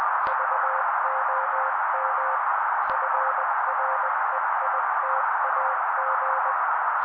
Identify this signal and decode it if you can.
Morse code, heard as a keyed tone, about 20 wpm, 530 Hz. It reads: VOM FREITAG
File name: daily / 18wpm noise